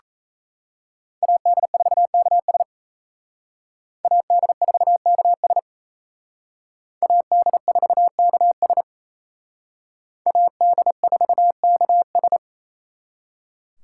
Morse code, heard as a keyed tone, about 36 wpm, 700 Hz. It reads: AD4KS AD4KS AD4KS AD4KS